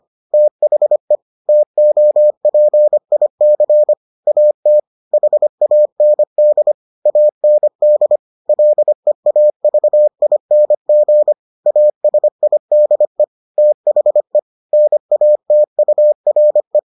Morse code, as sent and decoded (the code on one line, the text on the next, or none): - .... / - --- .--. .. -.-. / .- - / .... .- -. -.. / .- -. -.. / .-.. .- ...- .. -. --. / .- ... .. -.. / - .... / -. .- - ..- .-. .
THE TOPIC AT HAND AND LEAVING ASIDE THE NATURE